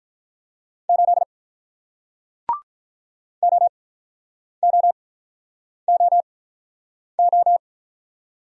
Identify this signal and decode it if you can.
Morse code, heard as a keyed tone, about 47 wpm, spaced as for 9 wpm, 700 Hz. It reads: 8 OOOO